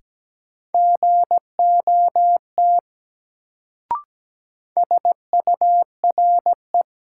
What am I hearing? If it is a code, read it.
Morse code, heard as a keyed tone, about 17 wpm, 700 Hz. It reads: GOT SURE
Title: Sentences from Top 300 Words - New Word Drill - 17wpm